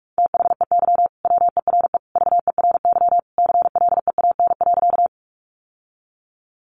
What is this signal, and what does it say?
Morse code, heard as a keyed tone, about 44 wpm, 700 Hz. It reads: THEY WERE VERY CLEAN.
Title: Sentences from Top 1100 Words 45wpm